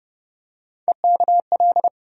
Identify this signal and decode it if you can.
Morse code, heard as a keyed tone, about 32 wpm, 700 Hz. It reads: EKL
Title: USA Call Sign Postfixes - 3 Letters 30wpm